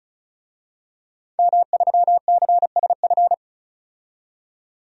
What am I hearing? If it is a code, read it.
Morse code, heard as a keyed tone, about 34 wpm, 700 Hz. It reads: M3CSF
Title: DXCC-100 Call Signs - 5 Characters - 35wpm